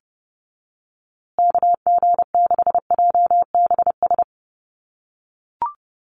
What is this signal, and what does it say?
Morse code, heard as a keyed tone, about 30 wpm, 700 Hz. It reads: KG6JBS
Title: USA Call Signs Repeated 3x 30wpm